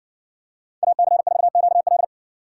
Morse code, heard as a keyed tone, about 60 wpm, 700 Hz. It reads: AC4QL